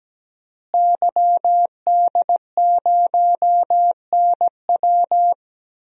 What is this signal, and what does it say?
Morse code, heard as a keyed tone, about 17 wpm, 700 Hz. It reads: YD0NW
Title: DXCC-100 Call Signs - 5 Characters - Repeated 3x - 17wpm